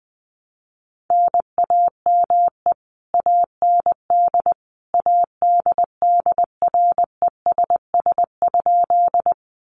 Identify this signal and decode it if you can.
Morse code, heard as a keyed tone, about 20 wpm, 700 Hz. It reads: NAME AND ADDRESS?
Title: Sentences from Top 1000 Words 20wpm